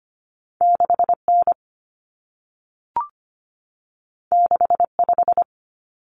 Morse code, heard as a keyed tone, about 25 wpm, 700 Hz. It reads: 6N 65
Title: Common Cut Numbers - 2-Digit 25wpm